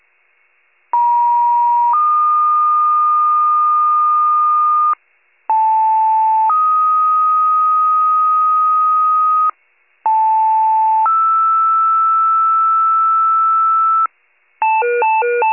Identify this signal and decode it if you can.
Two-tone page, heard as a one-second tone, then a three-second tone, about 950 then 1220 Hz, about 880 then 1250 Hz, about 880 then 1320 Hz. It is heard from 0.9 to 14.1 s.
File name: pager / 76 FIRE / wcalert